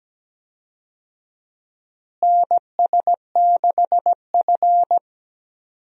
Morse code, heard as a keyed tone, about 17 wpm, 700 Hz. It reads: NS6F
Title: USA Call Signs - 4 Characters - Repeated 3x - 17wpm